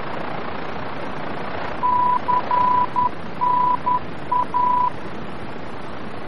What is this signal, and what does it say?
Morse code, heard as a keyed tone, about 10 wpm, 990 Hz, with noise. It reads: CNA
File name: france / mx / CNA-VOR-F